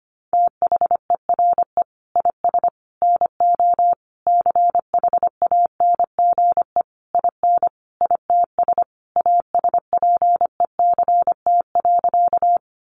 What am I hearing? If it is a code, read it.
Morse code, heard as a keyed tone, about 25 wpm, 700 Hz. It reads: THERE IS NO CHANGE IN ITS ASPECT.
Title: Sentences from Top 1300 Words 25wpm